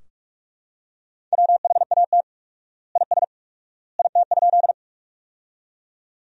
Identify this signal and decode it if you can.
Morse code, heard as a keyed tone, about 41 wpm, 700 Hz. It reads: WHAT IS IT?
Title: 4x Word-Spacing - Sentences from Top 100 Words - 45wpm